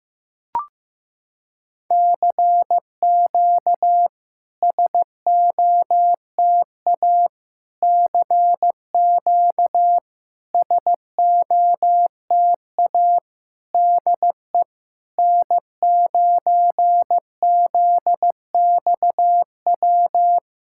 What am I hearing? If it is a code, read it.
Morse code, heard as a keyed tone, about 15 wpm, 700 Hz. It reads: CQ SOTA CQ SOTA DE N9ZXW
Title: SOTA QSO Practice 15wpm